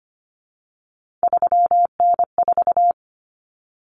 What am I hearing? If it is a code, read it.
Morse code, heard as a keyed tone, about 25 wpm, 700 Hz. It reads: 3N4
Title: Common Cut Numbers - 3-Digit 25wpm